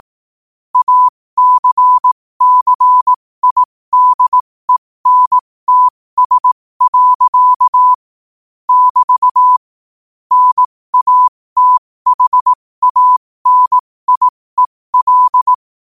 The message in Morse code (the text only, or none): ACCIDENTS. <BT> NATHANIEL